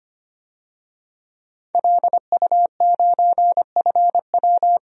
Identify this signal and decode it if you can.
Morse code, heard as a keyed tone, about 25 wpm, 700 Hz. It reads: LU9FW
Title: DXCC-100 Call Signs - 5 Characters - 25wpm